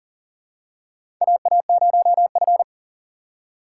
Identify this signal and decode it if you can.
Morse code, heard as a keyed tone, about 40 wpm, 700 Hz. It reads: AA0F
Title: USA Call Signs - 3x Character Spacing - 40wpm